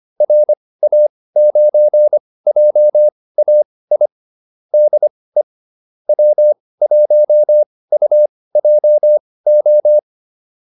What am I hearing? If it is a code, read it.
Morse code, heard as a keyed tone, about 25 wpm, spaced as for 18 wpm, 600 Hz. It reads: RA9JAI DE W1UJO